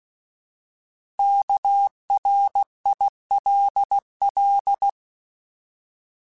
Morse code, heard as a keyed tone, about 16 wpm, 800 Hz. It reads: KRILL
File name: Alphabetical / Alphabetical mp3s / Ks @15